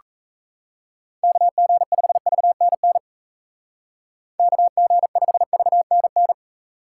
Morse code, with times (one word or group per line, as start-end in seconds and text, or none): 1.2-3.0 KG5VNN
4.4-6.3 KG5VNN